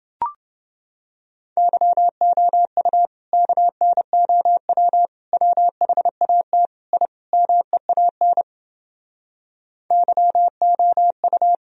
Morse code, heard as a keyed tone, about 31 wpm, 700 Hz. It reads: YOU KNOW WHAT I MEAN YOU